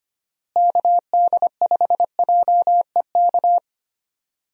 Morse code, heard as a keyed tone, about 25 wpm, 700 Hz. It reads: KD5JEK
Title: USA Call Signs - 6 Characters - Repeated 3x - 25wpm